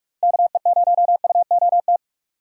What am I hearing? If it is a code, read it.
Morse code, heard as a keyed tone, about 44 wpm, 700 Hz. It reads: KE0UOT
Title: USA Call Signs - 6 Characters - 4x Character Spacing - 45wpm